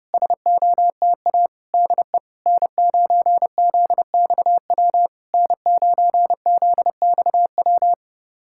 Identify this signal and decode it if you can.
Morse code, heard as a keyed tone, about 32 wpm, 700 Hz. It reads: SOTA DE N9ZXW N9ZXW
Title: SOTA QSO Practice 30wpm